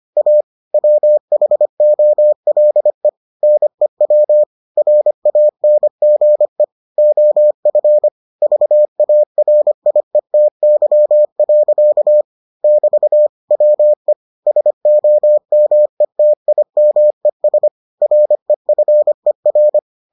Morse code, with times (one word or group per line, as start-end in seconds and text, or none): 0.2-0.4 A
0.7-3.1 WHOLE
3.4-4.4 NEW
4.8-6.6 RANGE
7.0-8.1 OF
8.4-12.2 VARIETY.
12.6-13.3 <BT>
13.5-14.1 WE
14.5-17.7 SOMETIMES
18.0-19.8 REFER